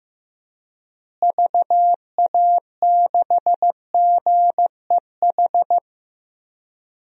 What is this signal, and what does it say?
Morse code, heard as a keyed tone, about 15 wpm, 700 Hz. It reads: VA6GEH
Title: DXCC-100 Call Signs - 6 Characters - Repeated 3x - 15wpm